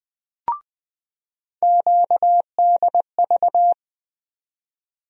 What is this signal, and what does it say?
Morse code, heard as a keyed tone, about 20 wpm, 700 Hz. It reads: QDV
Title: USA Call Sign Postfixes - 3 Letters 20wpm